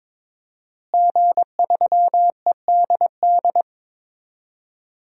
Morse code, heard as a keyed tone, about 21 wpm, 700 Hz. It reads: G3EDD
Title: DXCC-100 Call Signs - 5 Characters - Repeated 3x - 22wpm